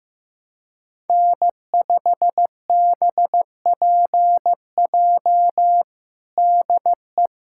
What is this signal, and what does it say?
Morse code, heard as a keyed tone, about 15 wpm, 700 Hz. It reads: N5BPJ DE
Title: K1USN SST QSO Practice 15wpm